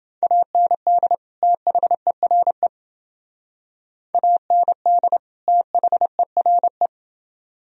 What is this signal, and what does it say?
Morse code, heard as a keyed tone, about 29 wpm, 700 Hz. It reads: AND THERE AND THERE